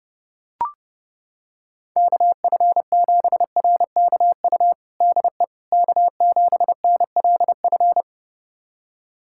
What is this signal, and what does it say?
Morse code, heard as a keyed tone, about 30 wpm, 700 Hz. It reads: KF7RKU DE K7NLF
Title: USA Callsign DE Callsign 30wpm